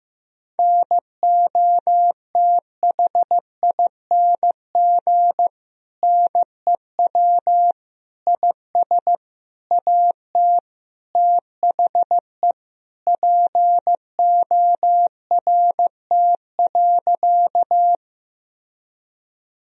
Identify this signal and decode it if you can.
Morse code, heard as a keyed tone, about 16 wpm, 700 Hz. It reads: NOTHING NEW IS AT THE PORT.
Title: Sentences from Top 1000 Words - Review Part 1 - 15wpm